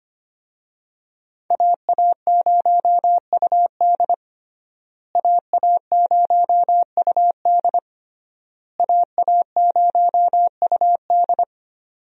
Morse code, heard as a keyed tone, about 25 wpm, 700 Hz. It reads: AA0UD AA0UD AA0UD
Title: USA Call Signs - 5 Characters - Repeated 3x - 25wpm